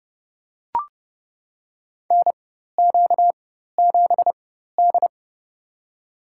Morse code, heard as a keyed tone, about 31 wpm, spaced as for 14 wpm, 700 Hz. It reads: NQ7D
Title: USA Call Signs - 4 Characters - 4x Character Spacing - 30wpm